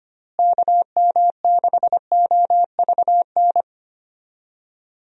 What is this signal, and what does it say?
Morse code, heard as a keyed tone, about 25 wpm, 700 Hz. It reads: KM6OVN
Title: USA Call Signs - 6 Characters - Repeated 2x - 25wpm